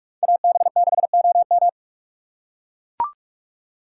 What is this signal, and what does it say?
Morse code, heard as a keyed tone, about 44 wpm, 700 Hz. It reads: AB6OM
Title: USA Call Signs - 6x Character Spacing - 45wpm